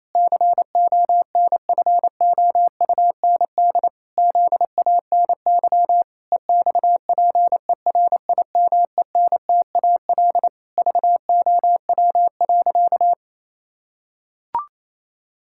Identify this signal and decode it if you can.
Morse code, heard as a keyed tone, about 28 wpm, 700 Hz. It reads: CONFOUND ZANY EXPERIMENTAL VOW.
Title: Pangram Challenge v2 - 28wpm